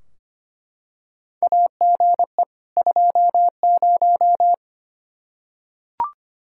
Morse code, heard as a keyed tone, about 25 wpm, 700 Hz. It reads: AGE 20
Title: Practice QSO - Exchange 3 - More Info - 25wpm